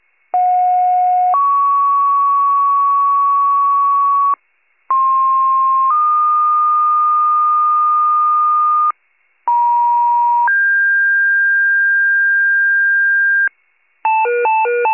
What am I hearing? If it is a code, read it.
Two-tone page, heard as a one-second tone, then a three-second tone, about 730 then 1120 Hz, about 1060 then 1220 Hz, about 950 then 1600 Hz. It is heard from 0.3 to 13.5 s.
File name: pager / 98 RESCUE / wcalert